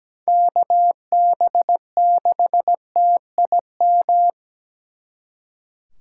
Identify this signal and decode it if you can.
Morse code, heard as a keyed tone, about 20 wpm, 700 Hz. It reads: KB6TIM
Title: Speed Racing - USA Call Signs 17wpm